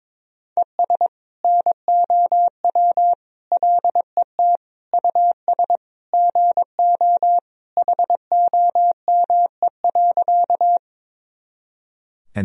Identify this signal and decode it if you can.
Morse code, heard as a keyed tone, about 22 wpm, 700 Hz. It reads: ES NOW LET US GO HOME.
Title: Sentences from Top 300 Words and Abbreviations 22wpm